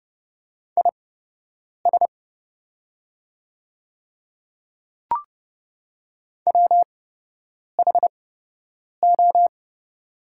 Morse code, heard as a keyed tone, about 30 wpm, spaced as for 8 wpm, 700 Hz. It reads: IS WHO
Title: 8x Character Spacing - Top 100 Words - 30wpm